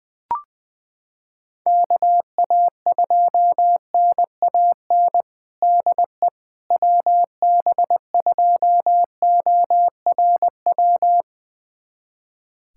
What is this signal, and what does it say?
Morse code, heard as a keyed tone, about 20 wpm, 700 Hz. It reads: KA2NAN DE WB2ORW
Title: USA Callsign DE Callsign 20wpm